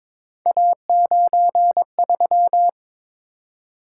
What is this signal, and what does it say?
Morse code, heard as a keyed tone, about 22 wpm, 700 Hz. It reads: A93
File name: practice / 22wpm / Three Letter-Number Combinations - 22wpm